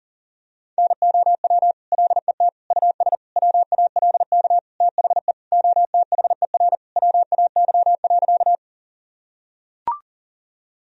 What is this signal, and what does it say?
Morse code, heard as a keyed tone, about 40 wpm, 700 Hz. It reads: NOW LET US WALK THE OTHER WAY.
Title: Sentences from Top 1000 Words - Review Part 1 - 40wpm